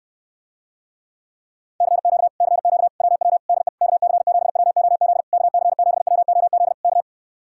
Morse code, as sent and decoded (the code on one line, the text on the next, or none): -.-. --.- / -.-. --.- / .-- .-- / -.. / -.- -.-. --... .-- --- --.. / -.- -.-. --... .-- --- --.. / -.-
CQ CQ WW DE KC7WOZ KC7WOZ K